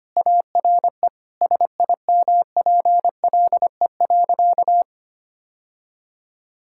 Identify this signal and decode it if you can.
Morse code, heard as a keyed tone, about 25 wpm, 700 Hz. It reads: ARE SIMPLE.
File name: practice / 25wpm / Sentences with 3-Word Phrases - 25wpm